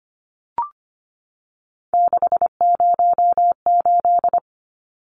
Morse code, heard as a keyed tone, about 25 wpm, 700 Hz. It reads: 608
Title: Numbers - 3-Digit 25wpm